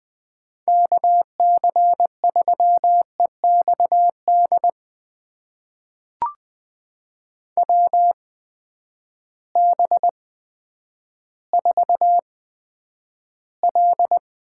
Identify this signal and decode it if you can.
Morse code, heard as a keyed tone, about 20 wpm, spaced as for 10 wpm, 700 Hz. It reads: KC3EXD WB4L
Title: USA Call Signs - 6 Characters - 8x Character Spacing - 20wpm